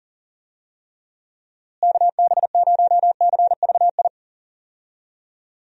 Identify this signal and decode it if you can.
Morse code, heard as a keyed tone, about 40 wpm, 700 Hz. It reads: KB0CVI
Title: USA Call Signs - 6 Characters - Repeated 2x - 40wpm